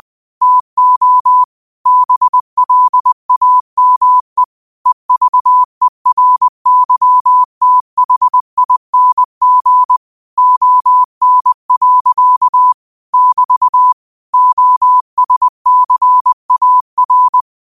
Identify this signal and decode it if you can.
Morse code, heard as a keyed tone, about 20 wpm, 1000 Hz. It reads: TO BLAME EVERYTHING ON. <BT> OSCAR
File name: qotdfiles / 20wpm